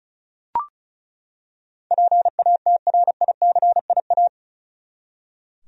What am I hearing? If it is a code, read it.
Morse code, heard as a keyed tone, about 34 wpm, 700 Hz. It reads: PATRICIA
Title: USA Common Names - 35wpm